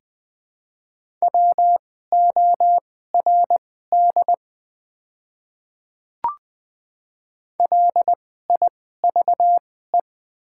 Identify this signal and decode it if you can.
Morse code, heard as a keyed tone, about 20 wpm, spaced as for 15 wpm, 690 Hz. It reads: WORD LIVE
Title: 2x Character Spacing - Top 500 Words - 20wpm